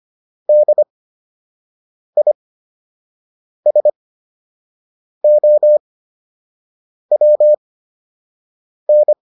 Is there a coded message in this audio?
Yes: Morse code from 0.5 to 9.1 s, heard as a keyed tone, about 25 wpm, spaced as for 6 wpm, 600 Hz. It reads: DISOWN